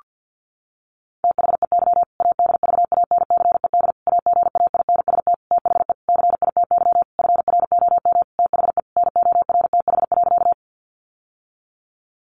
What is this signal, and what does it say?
Morse code, heard as a keyed tone, about 50 wpm, 700 Hz. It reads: THEY ADVANCED AGAINST THE CITY FROM THE NORTH.
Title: Sentences from Top 1200 Words 50wpm